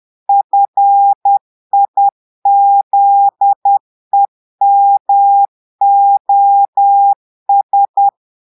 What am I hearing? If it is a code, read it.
Morse code, heard as a keyed tone, about 10 wpm, 800 Hz. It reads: FIZEMOS
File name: Portugais - texte en clair 1 - vitesse - 10 mots minute